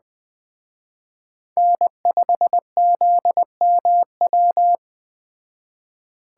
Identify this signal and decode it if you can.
Morse code, heard as a keyed tone, about 21 wpm, 700 Hz. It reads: N5ZMW